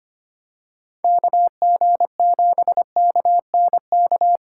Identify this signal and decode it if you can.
Morse code, heard as a keyed tone, about 25 wpm, 700 Hz. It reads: KG7KNK
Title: USA Callsign DE Callsign - 6 Characters - 25wpm